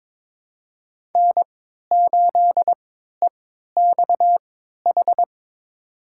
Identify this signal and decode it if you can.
Morse code, heard as a keyed tone, about 22 wpm, spaced as for 13 wpm, 700 Hz. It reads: N8EXH
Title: USA Call Signs - 5 Characters - 3x Character Spacing - 22wpm